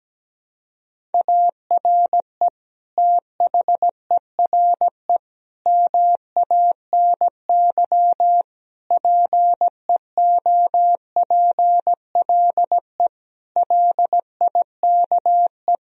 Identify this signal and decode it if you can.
Morse code, heard as a keyed tone, about 17 wpm, 700 Hz. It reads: ARE THERE MANY PEOPLE LIKE
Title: Sentences from Top 1000 Words - Review Part 1 - 17wpm